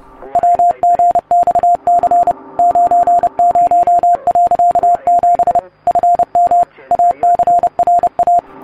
Morse code, heard as a keyed tone, about 30 wpm, 700 Hz. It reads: WGXC 90.7 FM ACRA